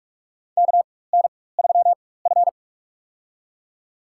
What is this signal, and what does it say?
Morse code, heard as a keyed tone, about 46 wpm, spaced as for 21 wpm, 700 Hz. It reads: KN3F
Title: USA Call Signs - 4 Characters - 4x Character Spacing - 45wpm